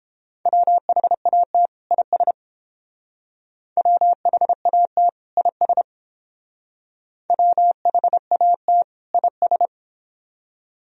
Morse code, heard as a keyed tone, about 29 wpm, 700 Hz. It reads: WHAT IS WHAT IS WHAT IS